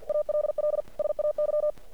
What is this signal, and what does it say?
Morse code, heard as a keyed tone, about 50 wpm, 600 Hz. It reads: ALL SAY
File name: archived / CRSnet